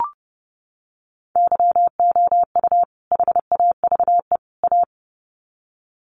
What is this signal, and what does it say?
Morse code, heard as a keyed tone, about 30 wpm, 700 Hz. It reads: YOU HAVE A
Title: Sets of 3 Words 30wpm